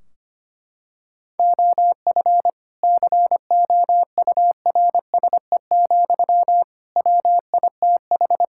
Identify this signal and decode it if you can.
Morse code, heard as a keyed tone, about 25 wpm, 700 Hz. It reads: OF COURSE, WITH
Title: Sentences from Top 1600 Words 25wpm